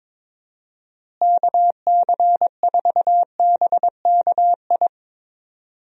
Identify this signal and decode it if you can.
Morse code, heard as a keyed tone, about 22 wpm, 700 Hz. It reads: KC4BKI
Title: USA Call Signs - 6 Characters - Repeated 3x - 22wpm